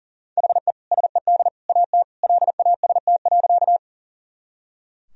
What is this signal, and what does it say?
Morse code, heard as a keyed tone, about 40 wpm, 700 Hz. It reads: HE SED AT LAST.
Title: Sentences from Top 300 Words and Abbreviations 40wpm